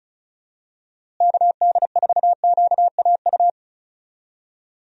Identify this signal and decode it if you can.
Morse code, heard as a keyed tone, about 34 wpm, 700 Hz. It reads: KD4QAU
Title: USA Call Signs - 6 Characters - Repeated 3x - 35wpm